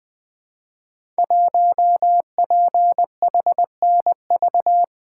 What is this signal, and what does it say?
Morse code, heard as a keyed tone, about 20 wpm, 700 Hz. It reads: 1PHNV
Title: Code Groups - 5 Letter:Numbers - 20wpm